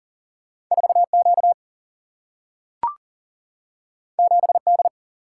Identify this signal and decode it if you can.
Morse code, heard as a keyed tone, about 42 wpm, 700 Hz. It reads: 4 Q 7 D
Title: Single-Letter Single-Number 40wpm